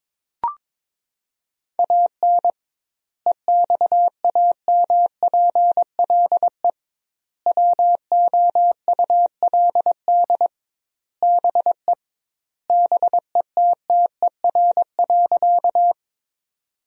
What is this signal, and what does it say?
Morse code, heard as a keyed tone, about 22 wpm, 700 Hz. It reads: AN EXAMPLE WOULD BE BETTER.
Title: 2x Word-Spacing - Sentences from Top 400 Words - 22wpm